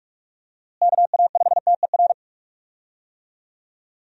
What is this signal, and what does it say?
Morse code, heard as a keyed tone, about 46 wpm, 700 Hz. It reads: KA5TER
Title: USA Call Signs - 6 Characters - Repeated 3x - 45wpm